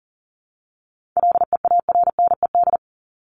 Morse code, heard as a keyed tone, about 40 wpm, 700 Hz. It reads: LEARNED
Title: ED Suffix - Top 100 Words with Base Words 40wpm